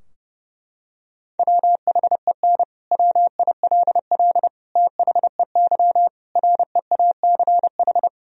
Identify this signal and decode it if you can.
Morse code, heard as a keyed tone, about 32 wpm, 700 Hz. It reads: WHEN WILL THEY REACH
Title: Sentences from Top 1000 Words - Review Part 4 - 30wpm